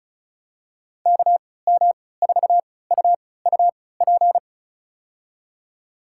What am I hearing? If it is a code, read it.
Morse code, heard as a keyed tone, about 36 wpm, spaced as for 20 wpm, 700 Hz. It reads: KM4UUP
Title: USA Call Signs - 3x Character Spacing - 35wpm